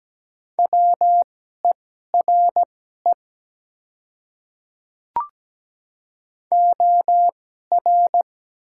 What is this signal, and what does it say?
Morse code, heard as a keyed tone, about 17 wpm, spaced as for 12 wpm, 700 Hz. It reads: WERE OR
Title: 2x Character Spacing - Top 200 Words - 17wpm